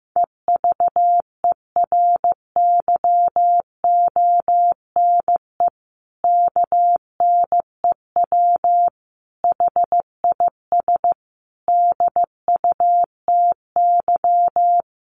Morse code, heard as a keyed tone, about 15 wpm, 700 Hz. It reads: EVERYONE KNEW HIS DUTY
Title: Sentences from Top 1500 Words 15wpm